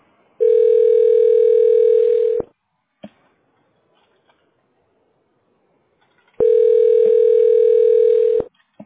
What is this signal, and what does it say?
Ringback tone (North America), 440 + 480 Hz.